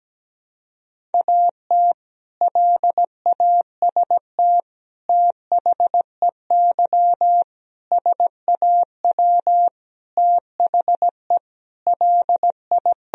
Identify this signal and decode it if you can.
Morse code, heard as a keyed tone, about 17 wpm, 700 Hz. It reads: AT LAST THEY SAW THE LI
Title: Sentences from Top 1000 Words - Review Part 1 - 17wpm